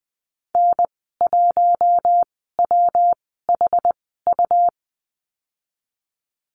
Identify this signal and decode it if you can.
Morse code, heard as a keyed tone, about 20 wpm, spaced as for 15 wpm, 700 Hz. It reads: N1WHU
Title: 154 Lesson 25g - Letter p - Call Signs 20wpm